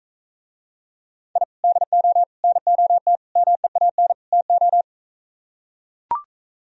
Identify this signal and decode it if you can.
Morse code, heard as a keyed tone, about 43 wpm, 700 Hz. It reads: I DO NOT MEAN TO